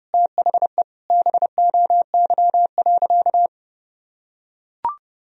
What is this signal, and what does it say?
Morse code, heard as a keyed tone, about 32 wpm, 700 Hz. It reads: THE BOY.
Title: Sentences from Top 1000 Words - Review Part 4 - 30wpm